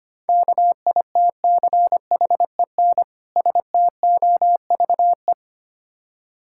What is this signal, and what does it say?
Morse code, heard as a keyed tone, about 26 wpm, 700 Hz. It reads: KITCHEN STOVE
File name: practice / 25wpm / Difficult Words - Speed-Racing with Hints - 25wpm